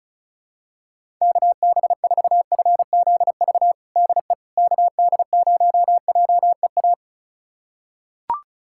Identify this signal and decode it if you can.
Morse code, heard as a keyed tone, about 36 wpm, 700 Hz. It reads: KB4FZV DE KD0JEA
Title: USA Callsign DE Callsign - 6 Characters - 35wpm